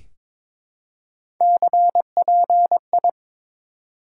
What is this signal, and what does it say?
Morse code, heard as a keyed tone, about 22 wpm, 700 Hz. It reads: CPI
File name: Top 90 QSO Elements - 4x Character Spacing - 22wpm